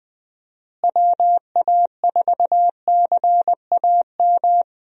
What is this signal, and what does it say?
Morse code, heard as a keyed tone, about 20 wpm, 700 Hz. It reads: WA4CAM
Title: USA Call Signs - 6 Characters - Repeated 2x - 20wpm